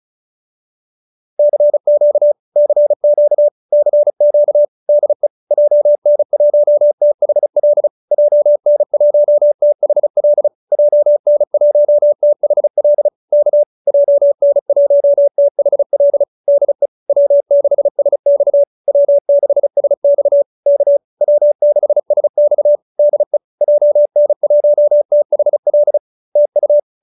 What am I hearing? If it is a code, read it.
Morse code, heard as a keyed tone, about 34 wpm, 590 Hz. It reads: CQ CQ CQ DE JN1THL JN1THL JN1THL K JN1THL DE W6SX W6SX K W6SX DE JN1THL TU